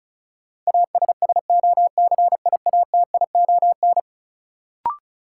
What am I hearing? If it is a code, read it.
Morse code, heard as a keyed tone, about 34 wpm, 700 Hz. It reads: ASSOCIATION
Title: Sentences from Top 1100 Words - New Word Drill Alt - 35wpm